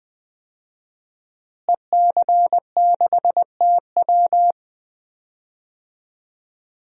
Morse code, heard as a keyed tone, about 20 wpm, 700 Hz. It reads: EC6TW
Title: DXCC-100 Call Signs - 5 Characters - Repeated 2x - 20wpm